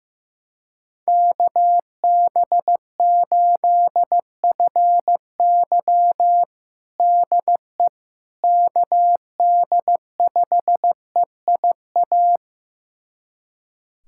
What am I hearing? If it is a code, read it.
Morse code, heard as a keyed tone, about 15 wpm, 700 Hz. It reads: KB8FY DE KD5EIA
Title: USA Callsign DE Callsign 15wpm